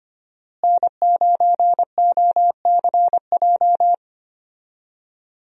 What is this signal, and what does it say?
Morse code, heard as a keyed tone, about 25 wpm, 700 Hz. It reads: N9OCJ
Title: USA Call Signs - 5 Characters - Repeated 3x - 25wpm